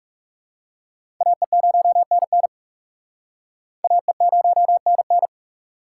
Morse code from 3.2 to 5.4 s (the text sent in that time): AE0NN